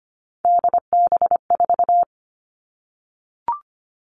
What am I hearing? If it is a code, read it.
Morse code, heard as a keyed tone, about 25 wpm, 700 Hz. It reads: DB4
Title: All Cut Numbers - 3-Digit 25wpm